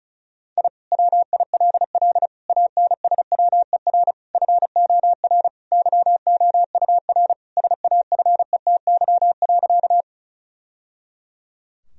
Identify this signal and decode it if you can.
Morse code, heard as a keyed tone, about 34 wpm, 700 Hz. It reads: I WILL ANSWER FOR YOUR SAFETY.
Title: Sentences from Top 1000 Words - Review Part 4 - 35wpm